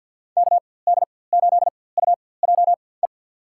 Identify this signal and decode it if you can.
Morse code, heard as a keyed tone, about 51 wpm, spaced as for 23 wpm, 700 Hz. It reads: KD8UJE
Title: USA Call Signs - 4x Character Spacing - 50wpm